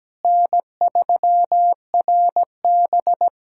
Morse code, heard as a keyed tone, about 17 wpm, 700 Hz. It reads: N3RB
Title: SOTA QSO Practice 17wpm